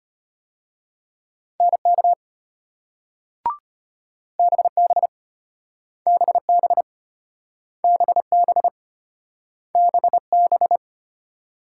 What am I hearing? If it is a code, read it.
Morse code, heard as a keyed tone, about 35 wpm, 700 Hz. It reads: NK BB BB BB BB